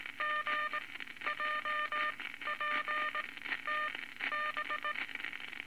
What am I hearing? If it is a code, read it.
Morse code, heard as a keyed tone, about 18 wpm, spaced as for 12 wpm, 1210 Hz, with noise. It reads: GJPTB
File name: Morse Code Records / 19 A LCM 166 Ten Words Per Minute